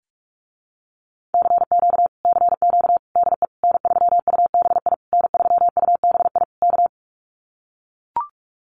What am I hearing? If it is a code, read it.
Morse code, heard as a keyed tone, about 44 wpm, 700 Hz. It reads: CQ CQ DE N3UBI N3UBI K
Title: Common QSO Phrases Encore 45wpm